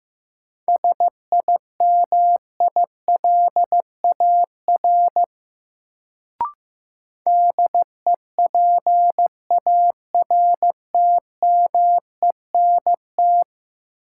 Morse code, heard as a keyed tone, about 15 wpm, 700 Hz. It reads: SIMILAR DEPARTMENT
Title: Sentences from Top 1000 Words - Review Part 5 - New Word Drill - 15wpm